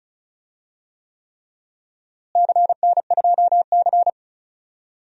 Morse code, heard as a keyed tone, about 34 wpm, 700 Hz. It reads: CN2C
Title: DXCC-100 Call Signs - 4 Characters - Repeated 3x - 35wpm